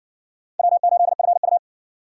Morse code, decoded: W8FU

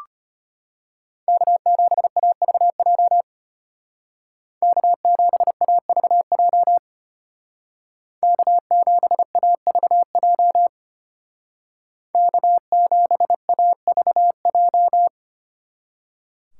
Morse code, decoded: K7AVJ K7AVJ K7AVJ K7AVJ

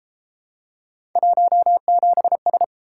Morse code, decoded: 17S

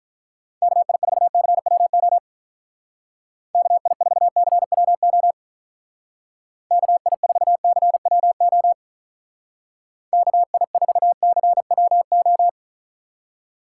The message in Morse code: KI4CWO KI4CWO KI4CWO KI4CWO

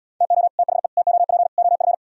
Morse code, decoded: TO THE TOP OF

